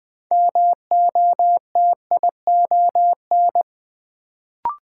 Morse code, heard as a keyed tone, about 20 wpm, 700 Hz. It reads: MOTION
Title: Build A Word - 3 Letters At A Time - 6-Letter Words - 20wpm